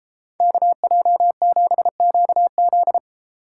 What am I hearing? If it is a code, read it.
Morse code, heard as a keyed tone, about 33 wpm, 700 Hz. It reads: KJ7QZ